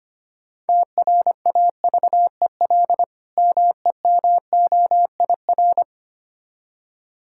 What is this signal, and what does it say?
Morse code, heard as a keyed tone, about 25 wpm, 690 Hz. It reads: TRAVEL MEMOIR
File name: practice / 25wpm / Difficult Words with Hints - 25wpm